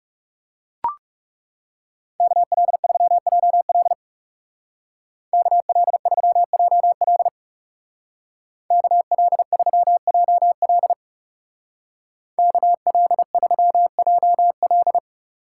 Morse code, heard as a keyed tone, about 37 wpm, 700 Hz. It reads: KL3JL KL3JL KL3JL KL3JL
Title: Speed Racing - USA 2x2 Call Signs 30wpm